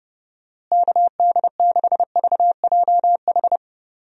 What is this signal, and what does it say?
Morse code, heard as a keyed tone, about 31 wpm, 700 Hz. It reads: KD6VJH
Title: USA Call Signs - 6 Characters - Repeated 2x - 30wpm